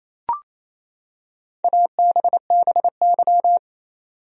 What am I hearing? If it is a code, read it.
Morse code, heard as a keyed tone, about 28 wpm, 700 Hz. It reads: ABBY
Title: USA Common Names - Max 4 Letters - 28wpm